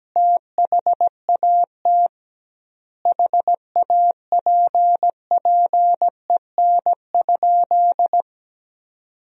Morse code, heard as a keyed tone, about 17 wpm, 700 Hz. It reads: THAT HAPPEN?
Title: 2x Word-Spacing - Sentences from Top 400 Words - 17wpm